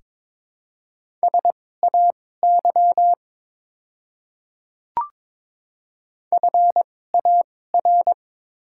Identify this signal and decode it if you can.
Morse code, heard as a keyed tone, about 22 wpm, spaced as for 16 wpm, 690 Hz. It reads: SAY FAR
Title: 2x Character Spacing - Top 300 Words - 22wpm